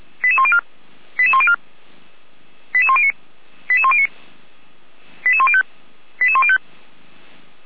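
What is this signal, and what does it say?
Five-tone selective call (ZVEI), heard as quick runs of five tones, tones of about 80 ms. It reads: 88104 88104 88109 88109 88105 88105
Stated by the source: ZVEI five-tone call 88405 received by SDR on the 455 channel.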